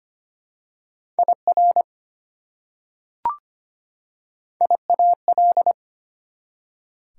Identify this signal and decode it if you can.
Morse code, heard as a keyed tone, about 25 wpm, 700 Hz. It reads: IR IAL
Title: Prefixes and Suffixes 25wpm